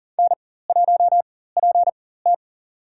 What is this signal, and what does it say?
Morse code, heard as a keyed tone, about 37 wpm, spaced as for 18 wpm, 690 Hz. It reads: N1PT
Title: USA Call Signs - 4x Character Spacing - 40wpm